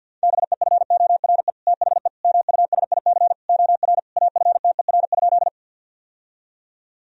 Morse code, heard as a keyed tone, about 49 wpm, 700 Hz. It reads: BEFORE THE MUSIC OR AFTER?
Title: Sentences from Top 1000 Words - Review Part 1 - 50wpm